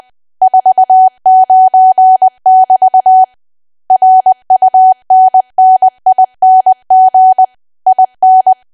Morse code, heard as a keyed tone, about 20 wpm, 750 Hz. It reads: 49 <BT> RUNNING IN
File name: cwa / W1AW-Level3 / 20WPM